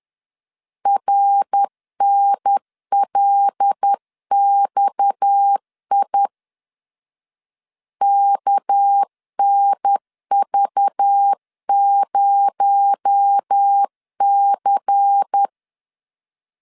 Morse code, heard as a keyed tone, about 10 wpm, 800 Hz. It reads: RNLXI KNV0C
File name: LEZIONI / 7 GF9Z / 7 GF 10wpm 2514